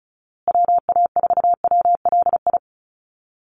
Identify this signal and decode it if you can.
Morse code, heard as a keyed tone, about 34 wpm, 700 Hz. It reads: WA4WLI
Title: Call Signs Repeated 2x - 35wpm